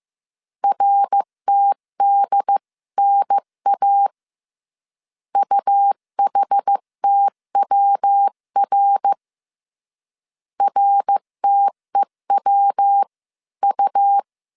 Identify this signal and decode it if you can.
Morse code, heard as a keyed tone, about 15 wpm, 800 Hz. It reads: RTDNA UHTWR RTEWU
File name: CW / DET / 1 3 1500C S C 14wpm 2458